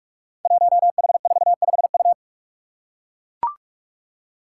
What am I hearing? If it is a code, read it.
Morse code, heard as a keyed tone, about 44 wpm, 700 Hz. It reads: 1H45U
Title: Five Challenging Letter-Number Combinations 45wpm